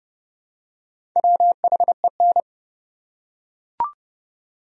Morse code, heard as a keyed tone, about 30 wpm, 700 Hz. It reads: WHEN
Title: Top 100 Words 30wpm